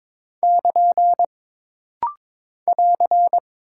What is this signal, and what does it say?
Morse code, heard as a keyed tone, about 22 wpm, 700 Hz. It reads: <KN> <AR>
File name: practice / 22wpm / Single Punctuation-Prosign - Rapid-Fire - 22wpm